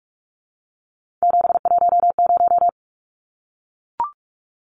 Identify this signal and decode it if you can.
Morse code, heard as a keyed tone, about 44 wpm, 700 Hz. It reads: 710
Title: Numbers - 3-Digit 45wpm